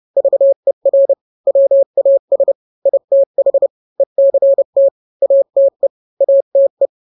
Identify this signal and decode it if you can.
Morse code, heard as a keyed tone, about 30 wpm, 550 Hz. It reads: VER WAS ITH ECT ATE ATE
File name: n-grams / 3-grams 30 25 WPM